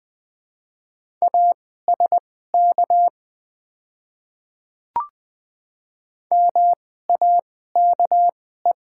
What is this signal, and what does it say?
Morse code, heard as a keyed tone, about 20 wpm, spaced as for 14 wpm, 690 Hz. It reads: ASK MAKE
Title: 2x Character Spacing - Top 500 Words - 20wpm